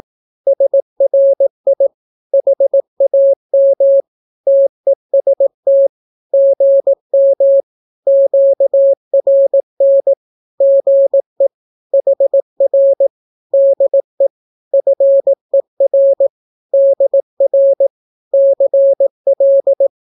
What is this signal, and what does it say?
Morse code, heard as a keyed tone, about 18 wpm, 550 Hz. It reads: SRI HAM TEST GM QRN GE HR DE FER DR CL
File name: naukapodstaw / skroty / skroty